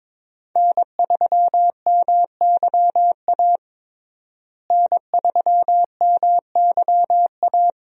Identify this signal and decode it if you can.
Morse code, heard as a keyed tone, about 22 wpm, 700 Hz. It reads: N3MYA N3MYA
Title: USA Call Signs - 5 Characters - Repeated 2x - 22wpm